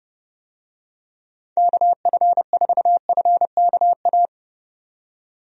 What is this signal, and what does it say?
Morse code, heard as a keyed tone, about 30 wpm, 700 Hz. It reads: KF4FKA
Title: USA Call Signs - 6 Characters - 30wpm